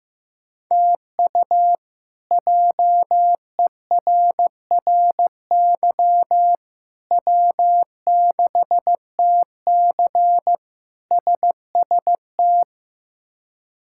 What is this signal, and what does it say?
Morse code, heard as a keyed tone, about 15 wpm, 700 Hz. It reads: TU JERRY W6TC SST
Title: K1USN SST QSO Practice - 15wpm